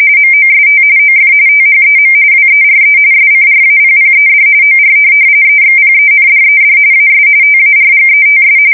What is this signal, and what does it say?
RTTY, about 45.45 baud. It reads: MOST OF THEM WERE SCRAPPED OR EXPENDED AS TARGETS
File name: fsk170 / USS-New-Jersey